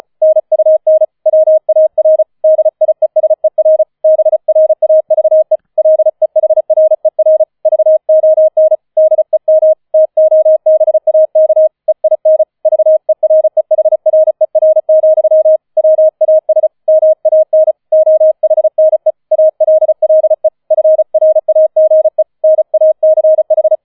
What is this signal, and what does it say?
Morse code, heard as a keyed tone, about 34 wpm, 630 Hz. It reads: NUN WAR DIESER BRAVE LEHRER VON DEM TOBAK EIN VEREHRER, WAS MAN OHNE ALLE FRAGE NACH
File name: 004 Vierter Streich 35wpm